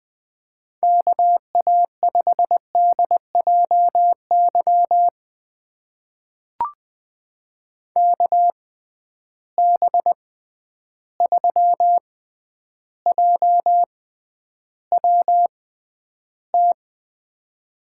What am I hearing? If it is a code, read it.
Morse code, heard as a keyed tone, about 20 wpm, spaced as for 10 wpm, 700 Hz. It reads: KA5DJY KB3JWT
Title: USA Call Signs - 6 Characters - 6x Character Spacing - 20wpm